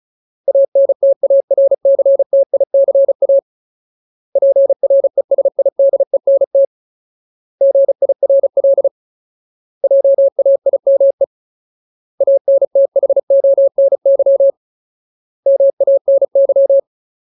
Morse code, heard as a keyed tone, about 34 wpm, 550 Hz. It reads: ANTARCTICA PRESIDENT GIRL JAIME ANTHONY MANY
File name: Ultimate Word List / 35WPM Ultimate